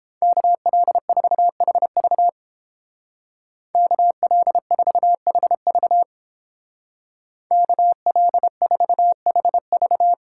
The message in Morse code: KL4HV KL4HV KL4HV

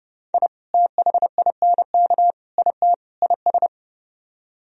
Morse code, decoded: I THINK IT IS

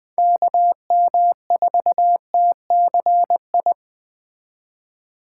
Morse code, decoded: KM4TCI

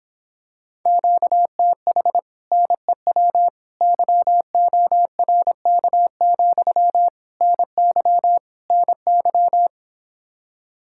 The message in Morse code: QTH NEW YORK, NY NY

